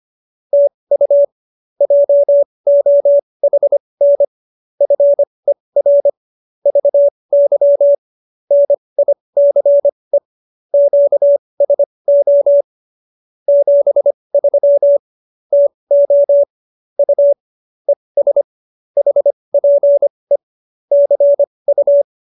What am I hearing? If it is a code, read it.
Morse code, heard as a keyed tone, about 25 wpm, spaced as for 20 wpm, 570 Hz. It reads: TU JOHN FER VY NICE QSO 73 TO U ES HPE CU